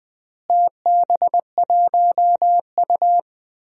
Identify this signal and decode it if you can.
Morse code, heard as a keyed tone, about 20 wpm, 700 Hz. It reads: TB1U
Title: DXCC-100 Call Signs - 4 Characters - Repeated 2x - 20wpm